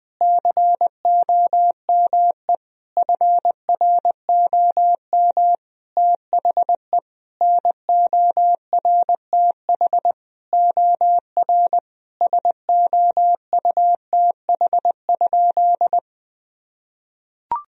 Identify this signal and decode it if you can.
Morse code, heard as a keyed tone, about 20 wpm, 700 Hz. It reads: COME FROM THE NORTH OR SOUTH?
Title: Sentences from Top 1000 Words - Review Part 2 - 20wpm